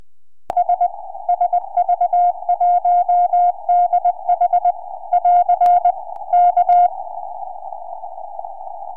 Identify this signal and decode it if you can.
Morse code, heard as a keyed tone, about 20 wpm, 710 Hz. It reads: S SV1DH <AR> K
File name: sv1dh qso 4m